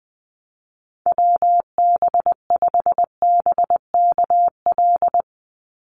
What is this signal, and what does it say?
Morse code, heard as a keyed tone, about 20 wpm, 700 Hz. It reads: WB5BKL
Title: NA SOTA Chaser Callsigns 20wpm